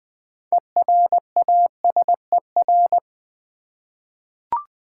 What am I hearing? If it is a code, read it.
Morse code, heard as a keyed tone, about 20 wpm, 700 Hz. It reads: ERASER
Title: Easy Words with Hints - 20wpm